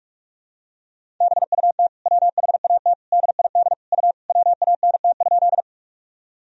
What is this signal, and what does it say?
Morse code, heard as a keyed tone, about 46 wpm, 700 Hz. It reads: BUT WHAT DID U WANT?
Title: Sentences from Top 200 Words and Abbreviations 45wpm